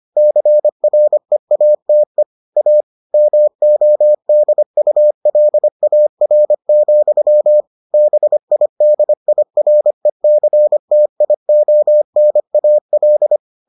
Morse code, heard as a keyed tone, about 25 wpm, 600 Hz. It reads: CREATE A MODULAR, BIDIRECTIONAL